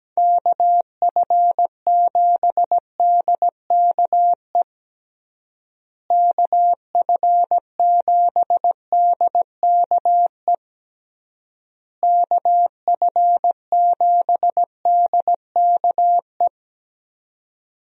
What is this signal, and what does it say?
Morse code, heard as a keyed tone, about 17 wpm, 700 Hz. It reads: KF7DKE KF7DKE KF7DKE
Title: USA Call Signs - 6 Characters - Repeated 3x - 17wpm